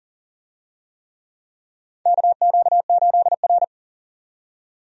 Morse code, heard as a keyed tone, about 40 wpm, 700 Hz. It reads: KQ8R
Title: USA Call Signs - 4 Characters - 40wpm